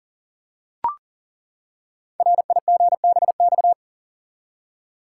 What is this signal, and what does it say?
Morse code, heard as a keyed tone, about 41 wpm, 690 Hz. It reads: RIGBX
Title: Code Groups - 5 Letters - 40wpm